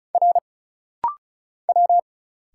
Morse code, heard as a keyed tone, about 34 wpm, 700 Hz. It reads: R W
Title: Single Letters - Rapid-Fire 35wpm